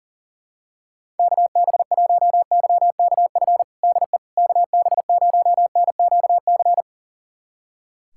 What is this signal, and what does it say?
Morse code, heard as a keyed tone, about 37 wpm, 700 Hz. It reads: KB1YKF DE KB0NQC
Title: USA Callsign DE Callsign - 6 Characters - 40wpm